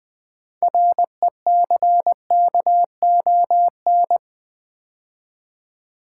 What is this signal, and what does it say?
Morse code, heard as a keyed tone, about 20 wpm, 700 Hz. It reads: RECKON
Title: 6 Letter Words - First 3 Letters Given - 20wpm